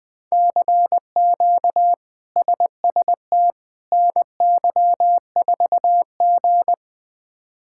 Morse code, heard as a keyed tone, about 20 wpm, 700 Hz. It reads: CQ SST NY4G
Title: K1USN SST QSO Practice - 20wpm